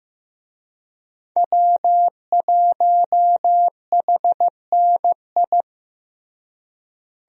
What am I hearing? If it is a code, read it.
Morse code, heard as a keyed tone, about 15 wpm, 700 Hz. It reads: W1HNI
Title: USA Call Signs - 5 Characters - Repeated 2x - 15wpm